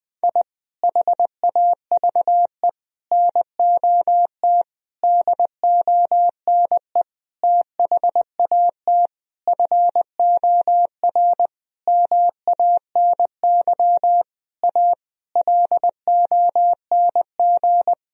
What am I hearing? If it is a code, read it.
Morse code, heard as a keyed tone, about 20 wpm, 700 Hz. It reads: I HAVE NOT DONE THAT FOR MANY A LONG
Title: Sentences from Top 1000 Words - Review Part 1 - 20wpm